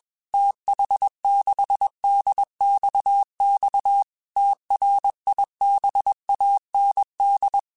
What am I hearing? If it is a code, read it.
Morse code, heard as a keyed tone, about 21 wpm, 800 Hz. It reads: TH6DXX TRIBAND
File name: QSOs / QSOs-mp3s / WD0BGP 1@20